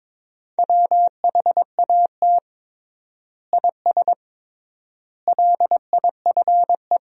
Morse code, heard as a keyed tone, about 22 wpm, 700 Hz. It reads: WHAT IS LIFE
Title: 3x Word-Spacing - Sentences from Top 300 Words - 22wpm